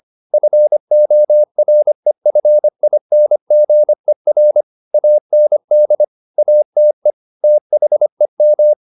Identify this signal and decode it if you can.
Morse code, heard as a keyed tone, about 25 wpm, 600 Hz. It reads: FOREFINGER AND ATE THEM